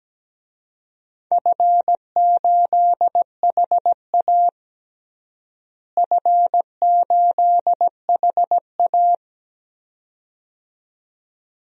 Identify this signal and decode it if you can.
Morse code, heard as a keyed tone, about 17 wpm, 700 Hz. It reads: F8HA F8HA
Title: DXCC-100 Call Signs - 4 Characters - Repeated 2x - 17wpm